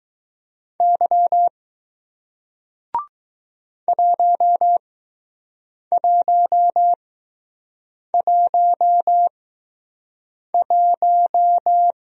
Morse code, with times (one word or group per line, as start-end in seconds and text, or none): 0.8-1.5 Y
3.9-11.9 1111